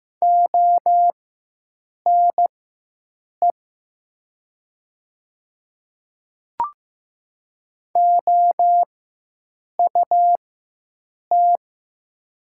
Morse code, heard as a keyed tone, about 15 wpm, spaced as for 7 wpm, 700 Hz. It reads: ONE OUT
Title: 4x Character Spacing - Top 100 Words - 15wpm